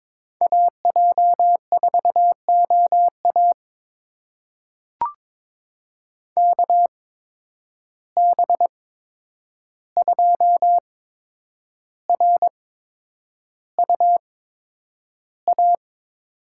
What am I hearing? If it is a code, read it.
Morse code, heard as a keyed tone, about 21 wpm, spaced as for 9 wpm, 700 Hz. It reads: AJ4OA KB2RUA
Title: USA Call Signs - 8x Character Spacing - 22wpm